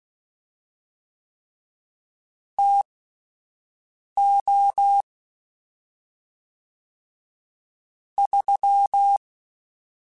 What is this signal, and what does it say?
Morse code, heard as a keyed tone, about 16 wpm, spaced as for 5 wpm, 800 Hz. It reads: TO 3
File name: Spaced Characters / Spaced Characters-mp3s / Weather 5 1 sp @10